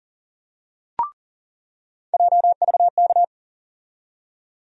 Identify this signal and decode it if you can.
Morse code, heard as a keyed tone, about 40 wpm, 700 Hz. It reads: JVK
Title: USA Call Sign Postfixes - 3 Letters 40wpm